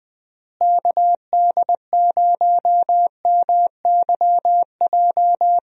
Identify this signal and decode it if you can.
Morse code, heard as a keyed tone, about 20 wpm, 700 Hz. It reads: KD0MYJ
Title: USA Call Signs - 6 Characters - Repeated 2x - 20wpm